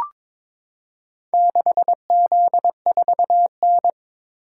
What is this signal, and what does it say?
Morse code, heard as a keyed tone, about 22 wpm, 700 Hz. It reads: 6Z4N
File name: practice / 22wpm / Four Letter-Number Combinations - 22wpm